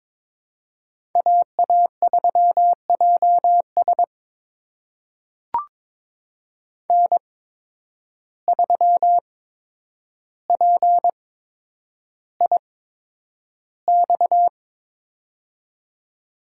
Morse code, heard as a keyed tone, about 22 wpm, spaced as for 9 wpm, 700 Hz. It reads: AA3JS N3PIX
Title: USA Call Signs - 5 Characters - 8x Character Spacing - 22wpm